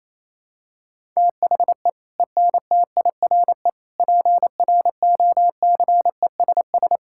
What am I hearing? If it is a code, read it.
Morse code, heard as a keyed tone, about 28 wpm, 700 Hz. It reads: THE ENTIRE PROCESS
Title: Sentences from Top 1000 Words - Review Part 5 - 28wpm